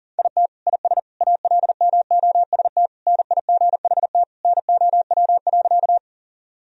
Morse code, heard as a keyed tone, about 40 wpm, 700 Hz. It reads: IT IS ALMOST NIGHT NOW.
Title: Sentences from Top 1000 Words - Review Part 4 - 40wpm